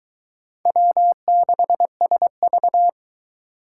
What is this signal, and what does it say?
Morse code, heard as a keyed tone, about 22 wpm, 700 Hz. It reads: W6SV